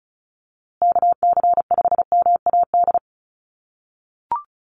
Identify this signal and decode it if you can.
Morse code, heard as a keyed tone, about 34 wpm, 700 Hz. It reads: KC5MAD
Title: USA Call Signs Repeated 3x 35wpm